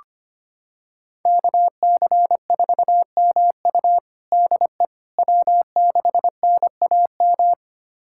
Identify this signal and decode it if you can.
Morse code, heard as a keyed tone, about 25 wpm, 700 Hz. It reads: KC4MU DE W6NAM
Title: USA Callsign DE Callsign - 5 Characters - 25wpm